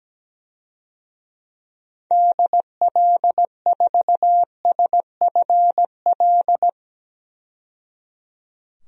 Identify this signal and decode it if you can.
Morse code, heard as a keyed tone, about 17 wpm, 700 Hz. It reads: DL4SFL
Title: DXCC-100 Call Signs - 6 Characters - 17wpm